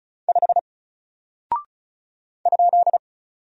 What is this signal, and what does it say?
Morse code, heard as a keyed tone, about 36 wpm, 700 Hz. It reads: 5 ?
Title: Single Character (Letter:Number:Punctuation:Prosign) Alt 35wpm